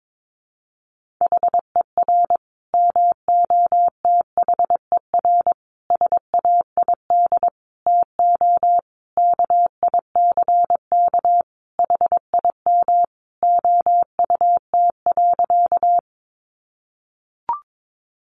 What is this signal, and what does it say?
Morse code, heard as a keyed tone, about 22 wpm, 700 Hz. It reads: HER MOTHER SAID TO KICK HIM OUT.
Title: Sentences from Top 1300 Words 22wpm